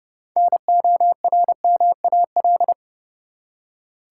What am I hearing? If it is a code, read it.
Morse code, heard as a keyed tone, about 30 wpm, 700 Hz. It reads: NORMAL
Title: Six Letter Words - 30wpm